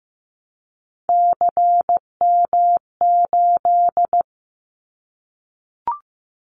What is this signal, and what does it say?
Morse code, heard as a keyed tone, about 15 wpm, 700 Hz. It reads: CM8
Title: Callsign Prefixes for Top 100 DXCC - 15wpm